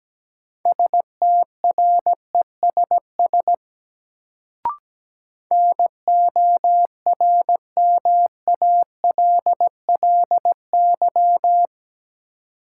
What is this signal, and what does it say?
Morse code, heard as a keyed tone, about 17 wpm, 700 Hz. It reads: STRESS NORMALLY
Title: Sentences from Top 1000 Words - Review Part 6 - New Word Drill - 17wpm